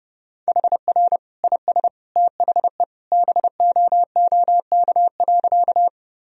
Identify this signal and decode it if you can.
Morse code, heard as a keyed tone, about 30 wpm, 700 Hz. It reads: HR IS THE BOOK.
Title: Sentences from Top 300 Words and Abbreviations 30wpm